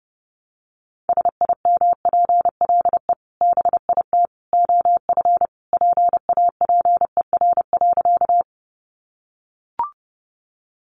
Morse code, heard as a keyed tone, about 30 wpm, 700 Hz. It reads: SIMPLE BIT OF PAPER.
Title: Sentences from Top 900 Words 30wpm